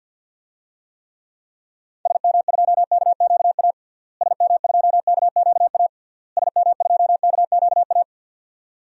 Morse code, heard as a keyed tone, about 51 wpm, 700 Hz. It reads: SM2KQA SM2KQA SM2KQA
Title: DXCC-100 Call Signs - 6 Characters - Repeated 3x - 50wpm